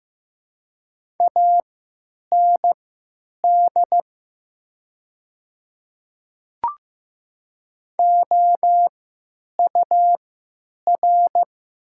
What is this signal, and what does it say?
Morse code, heard as a keyed tone, about 15 wpm, spaced as for 9 wpm, 700 Hz. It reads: AND OUR
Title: 3x Character Spacing - Top 200 Words - 15wpm